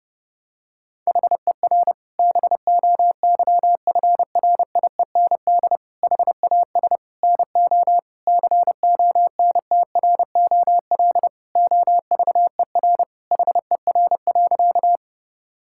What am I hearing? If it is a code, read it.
Morse code, heard as a keyed tone, about 30 wpm, 700 Hz. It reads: HER BOYFRIEND HAS NO CONTROL OVER HER.
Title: Sentences from Top 1000 Words - Review Part 5 - 30wpm